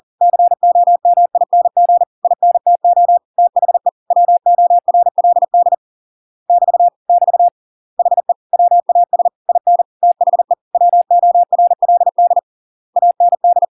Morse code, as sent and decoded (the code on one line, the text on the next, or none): -.-. --- -- .. -. --. / .. -. - --- / - .... / .-- --- .-. .-.. -.. / -...- / -...- / .... / .-- .- ... / .. -. / - .... / .-- --- .-. .-.. -.. / .- -. -..
COMING INTO THE WORLD <BT> <BT> HE WAS IN THE WORLD AND